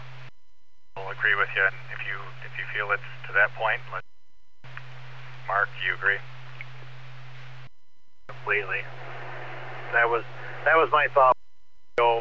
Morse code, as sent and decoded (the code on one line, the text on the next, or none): none